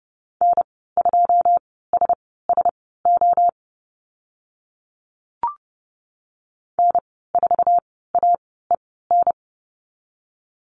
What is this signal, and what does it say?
Morse code, heard as a keyed tone, about 30 wpm, spaced as for 17 wpm, 700 Hz. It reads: N2SSO N4AEN
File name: Lesson 14g - Number 2 - Call Signs 30wpm